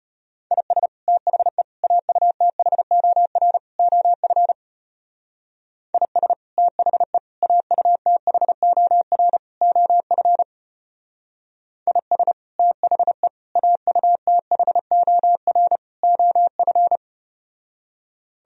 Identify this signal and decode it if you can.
Morse code, heard as a keyed tone, about 33 wpm, 700 Hz. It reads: IS THE AUTHOR OF IS THE AUTHOR OF IS THE AUTHOR OF